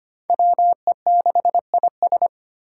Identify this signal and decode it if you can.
Morse code, heard as a keyed tone, about 25 wpm, 700 Hz. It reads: WE6IS
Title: Five Challenging Letter-Number Combinations 25wpm